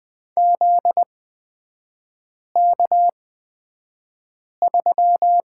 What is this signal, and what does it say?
Morse code, heard as a keyed tone, about 20 wpm, spaced as for 5 wpm, 700 Hz. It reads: ZK3
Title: 243 Lesson 39e - Letter z - Three Characters 20wpm